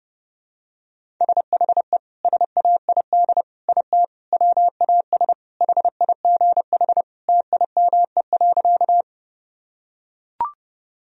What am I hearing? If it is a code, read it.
Morse code, heard as a keyed tone, about 29 wpm, 700 Hz. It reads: SHE SAID IT WAS HIGH TIME.
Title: Sentences from Top 1000 Words - Review Part 1 - 30wpm